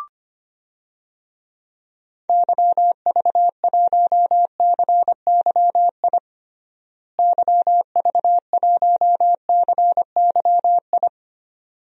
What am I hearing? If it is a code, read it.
Morse code, heard as a keyed tone, about 25 wpm, 700 Hz. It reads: YV1CYI YV1CYI
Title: DXCC-100 Call Signs - 6 Characters - Repeated 2x - 25wpm